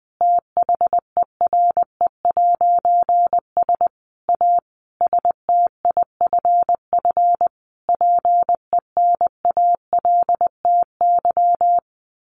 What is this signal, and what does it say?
Morse code, heard as a keyed tone, about 20 wpm, 700 Hz. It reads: THERE'S A STIFF PENALTY
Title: A Sound Of Thunder 20wpm